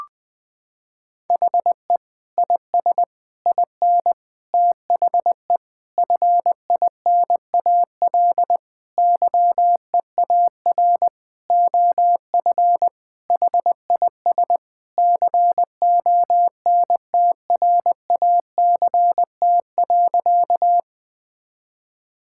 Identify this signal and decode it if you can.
Morse code, heard as a keyed tone, about 20 wpm, 700 Hz. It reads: HE IS IN THE FINAL YEAR OF HIS CONTRACT.
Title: Sentences with 8-Word Phrases - 20wpm